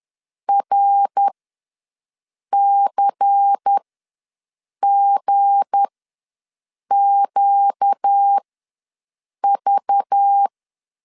Morse code, heard as a keyed tone, about 10 wpm, spaced as for 6 wpm, 800 Hz. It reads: RCGQV
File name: LEZIONI / 7 GF9Z / 7 G ws 10wpm 3843